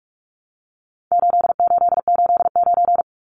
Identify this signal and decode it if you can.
Morse code, heard as a keyed tone, about 44 wpm, 700 Hz. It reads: 8889
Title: Numbers - 4-Digit 45wpm